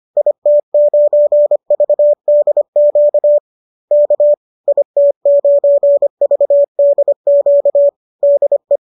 Morse code, heard as a keyed tone, about 25 wpm, 570 Hz. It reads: IT9VDQ K IT9VDQ DE